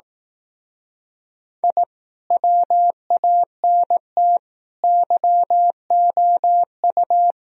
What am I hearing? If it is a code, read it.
Morse code, heard as a keyed tone, about 18 wpm, 700 Hz. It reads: I WANT YOU
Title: Speed-Racing - Sets of 4 Words 15wpm